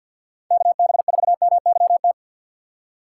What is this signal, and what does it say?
Morse code, heard as a keyed tone, about 65 wpm, spaced as for 58 wpm, 700 Hz. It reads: KB4MYT
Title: Speed Racing - USA 2x3 Call Signs 50wpm